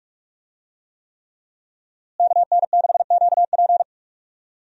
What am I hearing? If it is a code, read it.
Morse code, heard as a keyed tone, about 42 wpm, 700 Hz. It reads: KN6QP
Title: USA Call Signs - 5 Characters - Repeated 3x - 45wpm